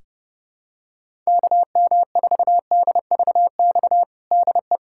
Morse code, heard as a keyed tone, about 30 wpm, 700 Hz. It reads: KM4DVX DE
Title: USA Callsign DE Callsign 30wpm